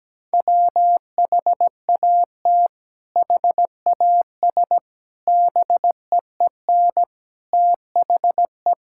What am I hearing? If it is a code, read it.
Morse code, heard as a keyed tone, about 17 wpm, 700 Hz. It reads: WHAT HAS BEEN THE
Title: Sentences from Top 1000 Words - Review Part 6 - 17wpm